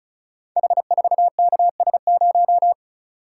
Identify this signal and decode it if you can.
Morse code, heard as a keyed tone, about 36 wpm, 700 Hz. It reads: H4KS0
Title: Code Groups - 5 Letter:Numbers - 35wpm